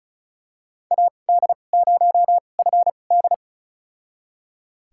Morse code, heard as a keyed tone, about 34 wpm, spaced as for 25 wpm, 700 Hz. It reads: AD0FD
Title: USA Call Signs - 5 Characters - 2x Character Spacing - 35wpm